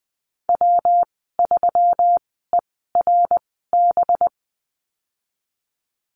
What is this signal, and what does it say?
Morse code, heard as a keyed tone, about 20 wpm, spaced as for 14 wpm, 700 Hz. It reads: W3ERB
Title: 207 Lesson 33g - Letter j - Call Signs 20wpm